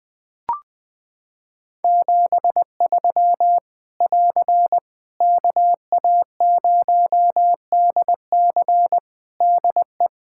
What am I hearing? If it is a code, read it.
Morse code, heard as a keyed tone, about 20 wpm, 700 Hz. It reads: 73 <AR> KA0DC DE
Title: K1USN SST QSO Practice 20wpm